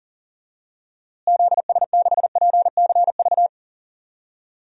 Morse code, heard as a keyed tone, about 40 wpm, 700 Hz. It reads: ZS6PCV
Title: DXCC-100 Call Signs - 6 Characters - Repeated 2x - 40wpm